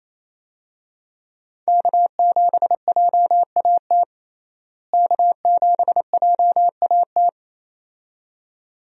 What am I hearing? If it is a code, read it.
Morse code, heard as a keyed tone, about 27 wpm, 700 Hz. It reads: K7JAT K7JAT
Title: USA Call Signs - 5 Characters - Repeated 2x - 28wpm